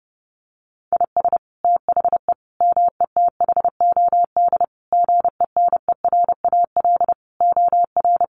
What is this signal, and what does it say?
Morse code, heard as a keyed tone, about 30 wpm, 700 Hz. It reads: IS THE METHOD GENERAL OR
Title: Sentences from Top 1500 Words 30wpm